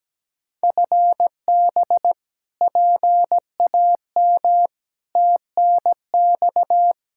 Morse code, heard as a keyed tone, about 17 wpm, 700 Hz. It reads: FB PAM TNX
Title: K1USN SST QSO Practice 17wpm